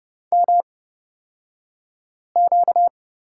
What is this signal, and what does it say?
Morse code, heard as a keyed tone, about 31 wpm, spaced as for 5 wpm, 700 Hz. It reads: MQ
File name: Lesson 38d - Letter q - Two Characters 30wpm